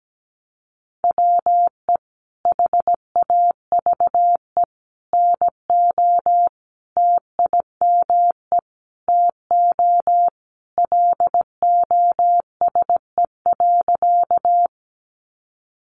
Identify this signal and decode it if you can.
Morse code, heard as a keyed tone, about 17 wpm, 700 Hz. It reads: WE HAVE NO TIME TO LOSE.
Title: Sentences from Top 1100 Words 17wpm